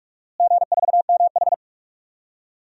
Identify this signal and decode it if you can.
Morse code, heard as a keyed tone, about 42 wpm, 700 Hz. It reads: G4MH
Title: DXCC-100 Call Signs - 4 Characters - Repeated 3x - 45wpm